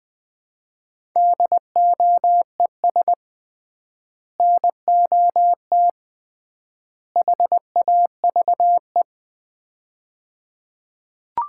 Morse code, heard as a keyed tone, about 20 wpm, 700 Hz. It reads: DOES NOT HAVE
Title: 3x Word-Spacing - Sets of 3 Words - 20wpm